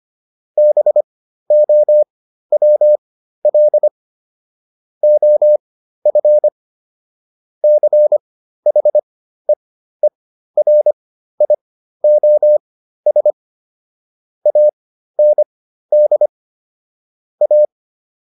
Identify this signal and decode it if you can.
Morse code, heard as a keyed tone, about 25 wpm, spaced as for 13 wpm, 600 Hz. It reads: BOWL OF CHEERIOS AND A